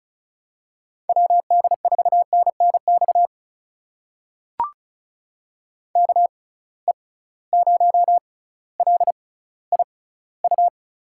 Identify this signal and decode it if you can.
Morse code, heard as a keyed tone, about 36 wpm, spaced as for 18 wpm, 700 Hz. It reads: WD4NNX KE0LIU
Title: USA Call Signs - 6 Characters - 6x Character Spacing - 35wpm